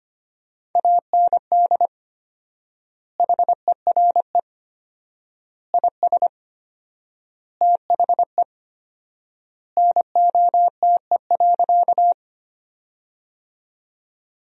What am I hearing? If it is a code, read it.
Morse code, heard as a keyed tone, about 25 wpm, 700 Hz. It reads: AND HERE IS THE NOTE.
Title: 4x Word-Spacing - Sentences from Top 500 Words - 25wpm